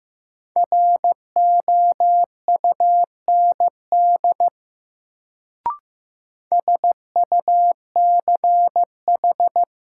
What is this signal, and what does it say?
Morse code, heard as a keyed tone, about 16 wpm, 700 Hz. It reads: ROUND SUCH
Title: Sentences from Top 200 Words - New Word Drill - 15wpm